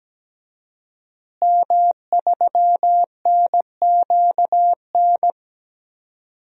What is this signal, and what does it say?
Morse code, heard as a keyed tone, about 17 wpm, 700 Hz. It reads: M3NQN
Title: DXCC-100 Call Signs - 5 Characters - Repeated 2x - 17wpm